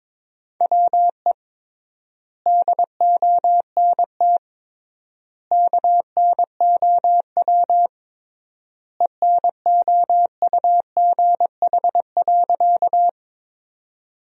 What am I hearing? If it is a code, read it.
Morse code, heard as a keyed tone, about 22 wpm, 700 Hz. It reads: WE DONT KNOW ENOUGH.
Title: 3x Word-Spacing - Sentences from Top 400 Words - 22wpm